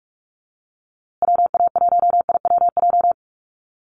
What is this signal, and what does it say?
Morse code, heard as a keyed tone, about 44 wpm, 700 Hz. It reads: WA1IWJ
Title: USA Call Signs Repeated 2x 45wpm